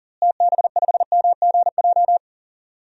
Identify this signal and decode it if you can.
Morse code, heard as a keyed tone, about 40 wpm, 700 Hz. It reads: TB5MGJ
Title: DXCC-100 Call Signs - 6 Characters - Repeated 3x - 40wpm